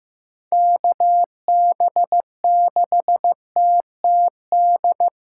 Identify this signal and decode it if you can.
Morse code, heard as a keyed tone, about 16 wpm, 700 Hz. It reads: KB6TTD
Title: USA Callsign DE Callsign - 6 Characters - 15wpm